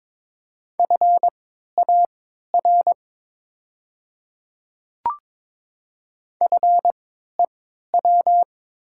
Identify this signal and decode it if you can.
Morse code, heard as a keyed tone, about 22 wpm, spaced as for 12 wpm, 700 Hz. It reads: FAR FEW
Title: 3x Character Spacing - Top 300 Words - 22wpm